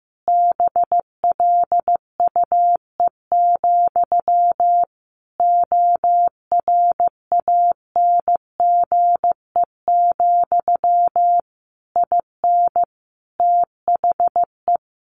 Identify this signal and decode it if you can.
Morse code, heard as a keyed tone, about 15 wpm, 700 Hz. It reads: BLUE, ORANGE, IN THE
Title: A Sound Of Thunder 15wpm